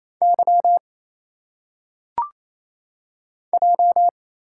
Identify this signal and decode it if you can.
Morse code, heard as a keyed tone, about 27 wpm, 700 Hz. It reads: Y J
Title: Single Letters - Phonetic Pronunciation 28wpm